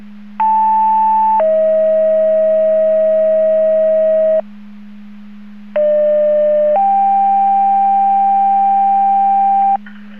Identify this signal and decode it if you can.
Two-tone page, heard as a one-second tone, then a three-second tone, about 880 then 640 Hz, about 600 then 790 Hz. It is heard from 0.4 to 9.8 s.
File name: Portage 154.385 Paging